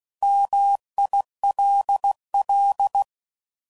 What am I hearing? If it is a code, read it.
Morse code, heard as a keyed tone, about 16 wpm, 800 Hz. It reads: MILL
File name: CW Code Groups @15WPM / fromSM's / Cities